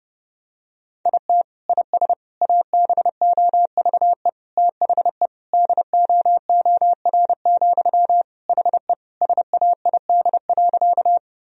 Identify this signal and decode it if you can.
Morse code, heard as a keyed tone, about 30 wpm, 700 Hz. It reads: IT IS ABOVE THE DOOR, HE SAID.
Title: Sentences from Top 1000 Words - Review Part 2 - 30wpm